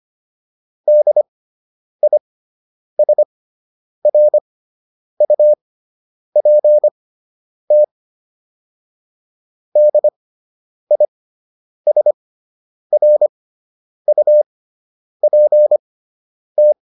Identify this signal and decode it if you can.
Morse code, heard as a keyed tone, about 25 wpm, spaced as for 9 wpm, 600 Hz. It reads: DISRUPT DISRUPT